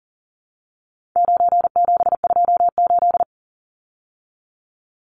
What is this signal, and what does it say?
Morse code, heard as a keyed tone, about 40 wpm, 700 Hz. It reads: 9728
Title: Numbers - 4-Digit 40wpm